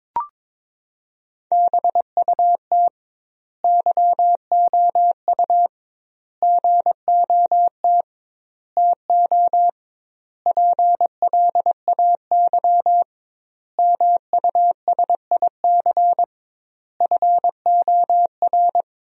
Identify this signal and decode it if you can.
Morse code, heard as a keyed tone, about 22 wpm, 700 Hz. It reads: BUT YOU GOT TO PLAY MUSIC FOR
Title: 2x Word-Spacing - Sentences from Top 300 Words - 22wpm